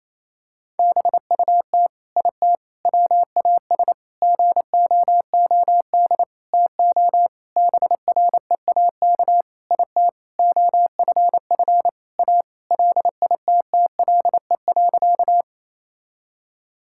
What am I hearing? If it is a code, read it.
Morse code, heard as a keyed tone, about 28 wpm, 700 Hz. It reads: BUT IT WAS GOOD TO BREAK IT OFF A LITTLE.
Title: Sentences from Top 1000 Words - Review Part 6 - 28wpm